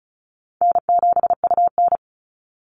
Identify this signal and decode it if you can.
Morse code, heard as a keyed tone, about 34 wpm, 700 Hz. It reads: N7UN
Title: NA SOTA Activator Callsigns 35wpm